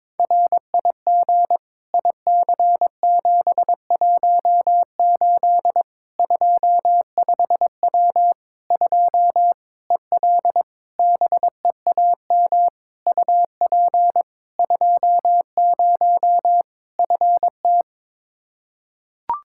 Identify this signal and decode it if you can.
Morse code, heard as a keyed tone, about 22 wpm, 700 Hz. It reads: RIG IC718 25W 2 EL BEAM UP 20 FT
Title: Practice QSO - Exchange 2 - Info - 22wpm